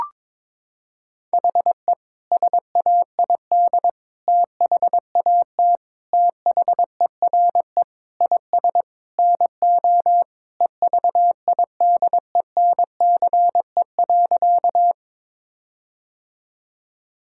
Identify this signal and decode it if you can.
Morse code, heard as a keyed tone, about 22 wpm, 700 Hz. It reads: HE SAID THAT THERE IS NO EVIDENCE.
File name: practice / 22wpm / Sentences with 4-Word Phrases - 22wpm